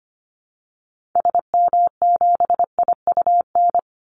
Morse code, heard as a keyed tone, about 25 wpm, 700 Hz. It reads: SM7IUN
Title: Call Signs Repeated 3x - 25wpm